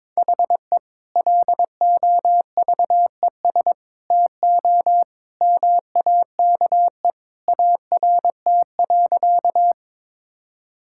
Morse code, heard as a keyed tone, about 21 wpm, 700 Hz. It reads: HE LOVES TO MAKE ART.
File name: practice / 22wpm / Sentences with 2-Word Phrases - 22wpm